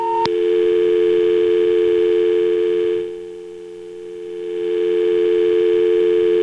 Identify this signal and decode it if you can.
Dial tone (North America), 350 + 440 Hz.